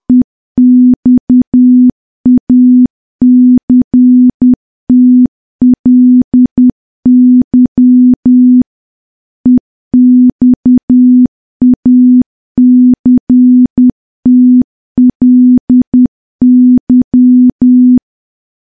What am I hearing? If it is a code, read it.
Morse code, heard as a keyed tone, about 10 wpm, 250 Hz. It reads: EXACTLY EXACTLY
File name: morse / morse-training-10WPM